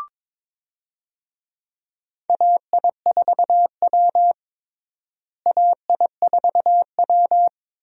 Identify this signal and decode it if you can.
Morse code, heard as a keyed tone, about 22 wpm, 700 Hz. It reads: AI4W AI4W
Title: USA Call Signs - 4 Characters - Repeated 2x - 22wpm